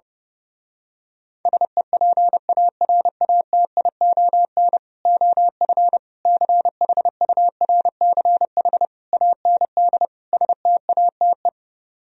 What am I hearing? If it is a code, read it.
Morse code, heard as a keyed tone, about 28 wpm, 700 Hz. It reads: SEPARATION OF CHURCH AND STATE
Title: Speed-Racing - Sets of 5 Words 30wpm